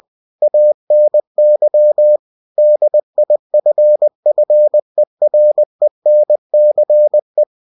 Morse code, heard as a keyed tone, about 20 wpm, 600 Hz. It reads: ANY DIFFERENCE
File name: cw-books / winnie-the-pooh / winnie-20wpm-page-0041